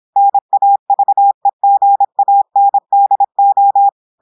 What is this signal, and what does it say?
Morse code, heard as a keyed tone, about 26 wpm, 800 Hz. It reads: NAVEGANDO
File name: audios / portugais / portugais texte en clair 1 vitesse 26 mots minutes